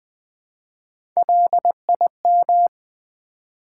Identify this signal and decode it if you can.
Morse code, heard as a keyed tone, about 20 wpm, 700 Hz. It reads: LIM
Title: Build A Word - 1 Letter At A Time - 3 to 5-Letter Words - 20wpm